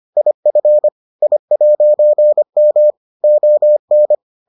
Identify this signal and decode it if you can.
Morse code, heard as a keyed tone, about 25 wpm, 600 Hz. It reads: IF I'M ON